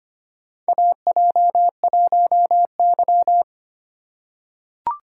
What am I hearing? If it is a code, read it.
Morse code, heard as a keyed tone, about 25 wpm, 700 Hz. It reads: AJ1Y
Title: USA Call Signs - 4 Characters - 4x Character Spacing - 25wpm